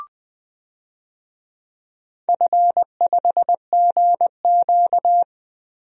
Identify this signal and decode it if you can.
Morse code, heard as a keyed tone, about 20 wpm, 700 Hz. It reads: F5GQ
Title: DXCC-100 Call Signs - 4 Characters - Repeated 3x - 20wpm